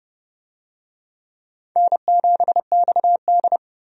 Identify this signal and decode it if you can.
Morse code, heard as a keyed tone, about 30 wpm, 700 Hz. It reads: N7XD
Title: USA Call Signs - 4 Characters - 30wpm